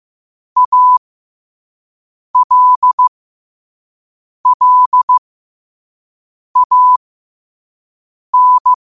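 Morse code, heard as a keyed tone, about 15 wpm, spaced as for 5 wpm, 1000 Hz. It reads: ALLAN
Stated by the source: Morse code quote of the day at 5 wpm.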